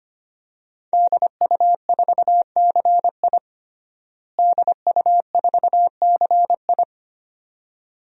Morse code, heard as a keyed tone, about 25 wpm, 700 Hz. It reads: DU4CI DU4CI
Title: DXCC-100 Call Signs - 5 Characters - Repeated 2x - 25wpm